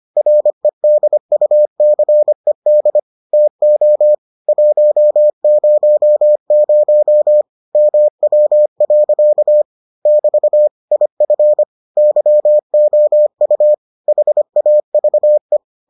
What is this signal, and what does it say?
Morse code, heard as a keyed tone, about 25 wpm, 600 Hz. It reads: REDUCED TO 100 MW. <BT> IF YOU HAVE